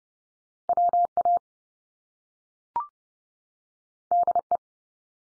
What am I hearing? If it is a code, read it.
Morse code, heard as a keyed tone, about 30 wpm, 700 Hz. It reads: WA DE
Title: US States 30WPM